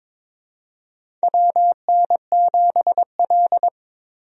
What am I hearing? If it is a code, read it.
Morse code, heard as a keyed tone, about 22 wpm, 700 Hz. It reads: WN7L
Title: USA Call Signs - 4 Characters - 22wpm